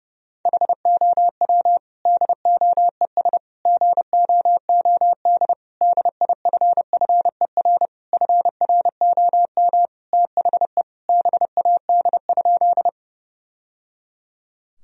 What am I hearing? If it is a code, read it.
Morse code, heard as a keyed tone, about 30 wpm, 700 Hz. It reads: HOW DOES GOOD DIFFER FROM THE BAD?
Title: Sentences from Top 1000 Words - Review Part 1 - 30wpm